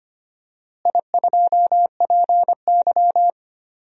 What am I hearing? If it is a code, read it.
Morse code, heard as a keyed tone, about 25 wpm, 700 Hz. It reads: I2PY
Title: DXCC-100 Call Signs - 4 Characters - Repeated 3x - 25wpm